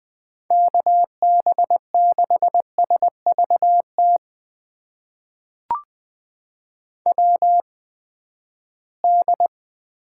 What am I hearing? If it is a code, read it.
Morse code, heard as a keyed tone, about 20 wpm, spaced as for 14 wpm, 700 Hz. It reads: KB6SVT WD
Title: USA Call Signs - 6 Characters - 8x Character Spacing - 20wpm